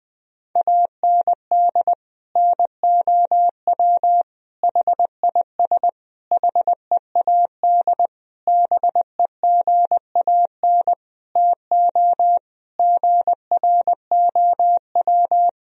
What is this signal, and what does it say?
Morse code, heard as a keyed tone, about 21 wpm, 700 Hz. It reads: AND NOW HIS HEAD BEGAN TO GROW
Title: Sentences from Top 1000 Words - Review Part 1 - 20wpm